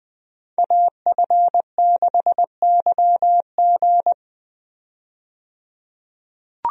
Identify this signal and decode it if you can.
Morse code, heard as a keyed tone, about 20 wpm, 700 Hz. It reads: AF6YG
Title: USA Call Signs - 5 Characters - 20wpm